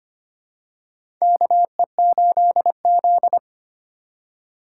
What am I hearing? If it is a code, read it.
Morse code, heard as a keyed tone, about 25 wpm, 700 Hz. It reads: KE8Z
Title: USA Call Signs - 4 Characters - 3x Character Spacing - 25wpm